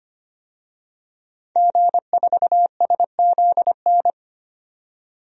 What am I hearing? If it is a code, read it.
Morse code, heard as a keyed tone, about 25 wpm, 700 Hz. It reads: G4SZN
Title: DXCC-100 Call Signs - 5 Characters - 25wpm